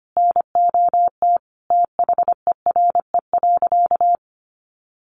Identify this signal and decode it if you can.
Morse code, heard as a keyed tone, about 25 wpm, 700 Hz. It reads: NOT THERE.
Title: Sentences from Top 1400 Words 25wpm